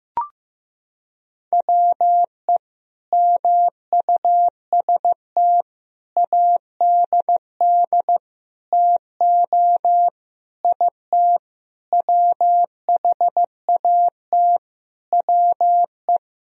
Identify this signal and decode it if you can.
Morse code, heard as a keyed tone, about 15 wpm, 700 Hz. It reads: WE MUST ADD TO IT WHAT WE
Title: Sentences from Top 1000 Words - Review Part 1 - 15wpm